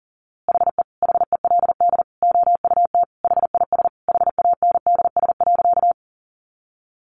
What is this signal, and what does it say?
Morse code, heard as a keyed tone, about 40 wpm, 700 Hz. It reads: HE HELD OUT HIS HANDS.
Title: A Sound Of Thunder 40wpm